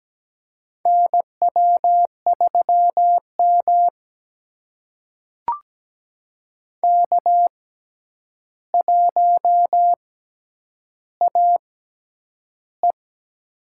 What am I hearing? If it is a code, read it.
Morse code, heard as a keyed tone, about 17 wpm, spaced as for 9 wpm, 700 Hz. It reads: NW3M K1AE
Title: USA Call Signs - 4 Characters - 6x Character Spacing - 17wpm